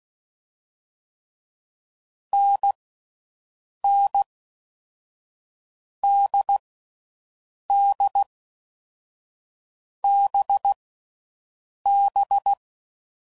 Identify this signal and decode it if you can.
Morse code, heard as a keyed tone, about 16 wpm, 800 Hz. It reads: N N D D B B
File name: Combinations / Difficult Combos / NDB NDB6